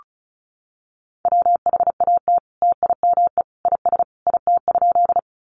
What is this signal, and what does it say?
Morse code, heard as a keyed tone, about 34 wpm, 700 Hz. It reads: WHAT TIME IS IT?
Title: Sentences from Top 100 Words 35wpm